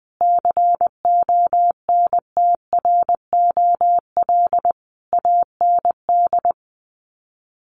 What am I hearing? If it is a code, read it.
Morse code, heard as a keyed tone, about 19 wpm, 700 Hz. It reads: CONTROL AND
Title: Sets of 5 Words Encore2 20wpm